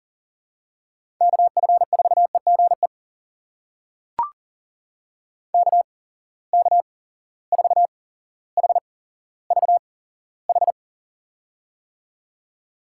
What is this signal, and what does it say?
Morse code, heard as a keyed tone, about 40 wpm, spaced as for 17 wpm, 700 Hz. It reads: KF4EGE KK4HVH
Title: USA Call Signs - 6 Characters - 8x Character Spacing - 40wpm